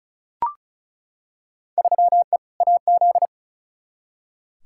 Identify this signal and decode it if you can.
Morse code, heard as a keyed tone, about 34 wpm, 700 Hz. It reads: 3E AZ
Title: Field Day Exchanges - 35wpm